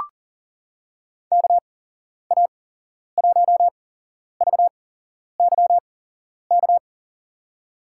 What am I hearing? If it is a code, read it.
Morse code, heard as a keyed tone, about 39 wpm, spaced as for 11 wpm, 700 Hz. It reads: KA1VYK
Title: USA Call Signs - 6 Characters - 8x Character Spacing - 40wpm